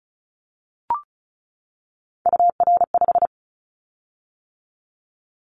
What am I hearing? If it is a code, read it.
Morse code, heard as a keyed tone, about 36 wpm, 700 Hz. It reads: UR5
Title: Callsign Prefixes for Top 25 DXCC - 35wpm